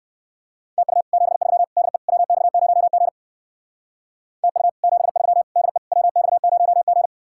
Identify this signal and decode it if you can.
Morse code, heard as a keyed tone, about 52 wpm, 700 Hz. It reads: TU 73 DE WX0G TU 73 DE WX0G
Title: Speed-Racing Common QSO Phrases Encore - 45wpm